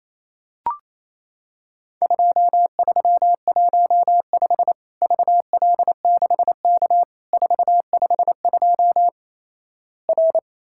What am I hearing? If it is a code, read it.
Morse code, heard as a keyed tone, about 28 wpm, 700 Hz. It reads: 2315 VL6K 452 R
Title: Worked All Europe DX Contest - QTC Exchange - 28wpm